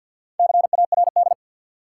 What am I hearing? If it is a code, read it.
Morse code, heard as a keyed tone, about 50 wpm, 700 Hz. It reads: CARD
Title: Sentences from Top 600 Words - New Word Drill - 50wpm